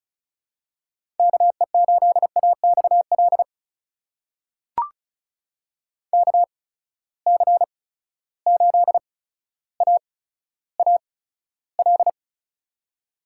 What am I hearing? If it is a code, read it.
Morse code, heard as a keyed tone, about 36 wpm, spaced as for 15 wpm, 700 Hz. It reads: KE8AXL KC8AAL
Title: USA Call Signs - 6 Characters - 8x Character Spacing - 35wpm